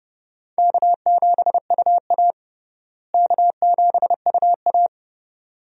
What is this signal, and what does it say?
Morse code, heard as a keyed tone, about 30 wpm, 700 Hz. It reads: K7UA K7UA
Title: USA Call Signs - 4 Characters - Repeated 2x - 30wpm